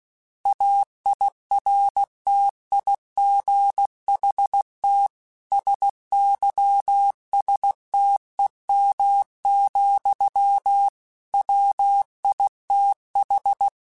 Morse code, heard as a keyed tone, about 16 wpm, 800 Hz. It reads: AIRTIGHT SYSTEM, WITH
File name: Quotes / Quotes-mp3s / Airtight @15